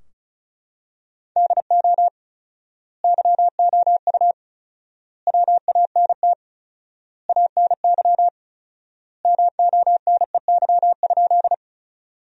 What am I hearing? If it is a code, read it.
Morse code, heard as a keyed tone, about 36 wpm, 700 Hz. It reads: DO YOU WANT ANY MONEY?
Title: 4x Word-Spacing - Sentences from Top 400 Words - 35wpm